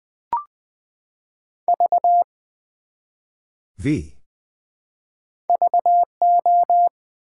Morse code, heard as a keyed tone, about 20 wpm, 700 Hz. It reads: V VO
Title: Build A Word - 1 Letter At A Time - 3-Letter Words - 20wpm